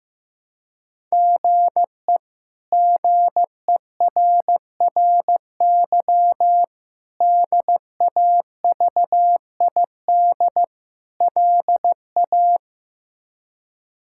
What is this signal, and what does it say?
Morse code, heard as a keyed tone, about 15 wpm, 700 Hz. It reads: GE GERRY DAVID LA
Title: K1USN SST QSO Practice - 15wpm